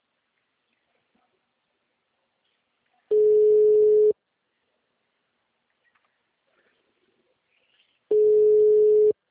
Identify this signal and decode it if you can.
Ringback tone (Europe, 425 Hz).
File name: out- -2012- - - .8218